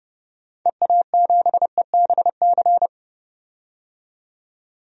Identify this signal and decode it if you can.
Morse code, heard as a keyed tone, about 30 wpm, 700 Hz. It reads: EA7EBC
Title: DXCC-100 Call Signs - 6 Characters - 30wpm